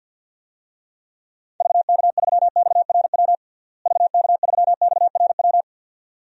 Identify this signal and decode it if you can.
Morse code, heard as a keyed tone, about 50 wpm, 700 Hz. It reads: VK3XRW VK3XRW
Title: DXCC-100 Call Signs - 6 Characters - Repeated 2x - 50wpm